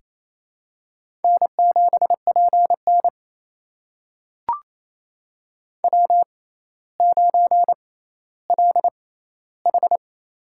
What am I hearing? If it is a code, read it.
Morse code, heard as a keyed tone, about 28 wpm, spaced as for 14 wpm, 700 Hz. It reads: N7PN W9LH
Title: USA Call Signs - 4 Characters - 6x Character Spacing - 28wpm